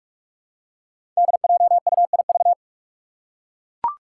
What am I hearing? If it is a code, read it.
Morse code, heard as a keyed tone, about 41 wpm, 700 Hz. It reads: DJUIV
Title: Five Challenging Letter-Number Combinations 45wpm